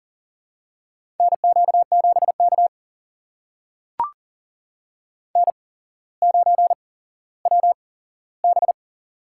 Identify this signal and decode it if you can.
Morse code, heard as a keyed tone, about 40 wpm, spaced as for 17 wpm, 700 Hz. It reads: NQ7K N9WB
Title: USA Call Signs - 4 Characters - 8x Character Spacing - 40wpm